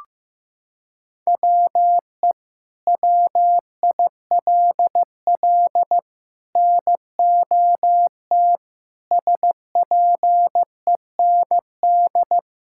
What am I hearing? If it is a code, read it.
Morse code, heard as a keyed tone, about 15 wpm, 700 Hz. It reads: WE WILL NOT SPEND
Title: Sentences from Top 1000 Words - Review Part 5 - 15wpm